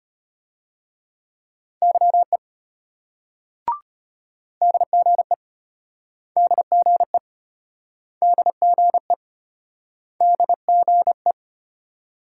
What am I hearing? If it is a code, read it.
Morse code, heard as a keyed tone, about 35 wpm, 700 Hz. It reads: YE DGE DGE DGE DGE